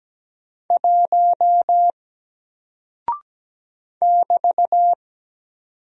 Morse code, heard as a keyed tone, about 17 wpm, 700 Hz. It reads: 1 <BT>
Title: Single Character (Letter:Number:Punctuation:Prosign) Alt 17wpm